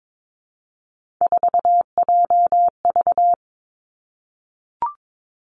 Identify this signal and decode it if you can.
Morse code, heard as a keyed tone, about 22 wpm, 700 Hz. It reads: 4JV
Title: Three Challenging Letter-Number Combinations 22wpm